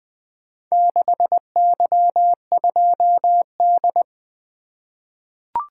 Morse code, heard as a keyed tone, about 21 wpm, 700 Hz. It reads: 6Y2D
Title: Four Letter-Number Combinations - 20wpm